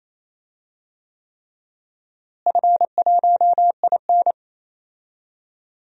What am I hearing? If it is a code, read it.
Morse code, heard as a keyed tone, about 27 wpm, 700 Hz. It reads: F1IN
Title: DXCC-100 Call Signs - 4 Characters - Repeated 3x - 28wpm